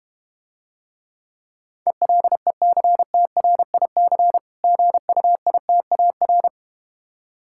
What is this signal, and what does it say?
Morse code, heard as a keyed tone, about 32 wpm, 700 Hz. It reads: ELECTRIC GUITAR